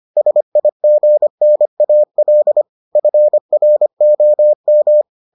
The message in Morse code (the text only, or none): SIGNAL FROM